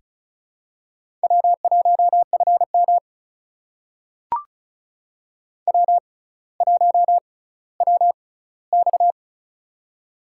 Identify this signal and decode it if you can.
Morse code, heard as a keyed tone, about 36 wpm, spaced as for 18 wpm, 700 Hz. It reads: W1FM W1WX
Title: USA Call Signs - 6x Character Spacing - 35wpm